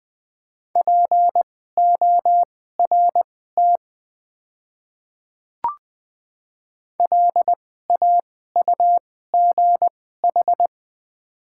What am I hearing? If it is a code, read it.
Morse code, heard as a keyed tone, about 20 wpm, spaced as for 14 wpm, 700 Hz. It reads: PORT LAUGH
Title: 2x Character Spacing - Top 500 Words - 20wpm